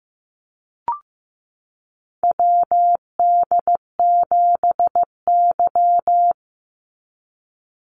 Morse code, heard as a keyed tone, about 15 wpm, 700 Hz. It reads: WD7Y
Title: NA SOTA Chaser Callsigns 15wpm